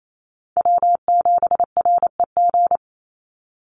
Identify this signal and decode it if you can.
Morse code, heard as a keyed tone, about 28 wpm, 700 Hz. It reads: W7REG
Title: USA Call Signs 28wpm